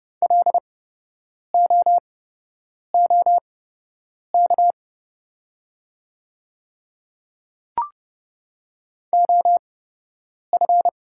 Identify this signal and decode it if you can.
Morse code, heard as a keyed tone, about 30 wpm, spaced as for 8 wpm, 700 Hz. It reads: LOOK OF
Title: 8x Character Spacing - Top 100 Words - 30wpm